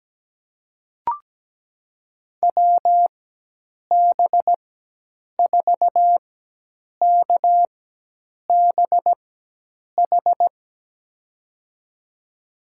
Morse code, heard as a keyed tone, about 17 wpm, spaced as for 8 wpm, 700 Hz. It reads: WB4KBH
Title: USA Call Signs - 6 Characters - 4x Character Spacing - 17wpm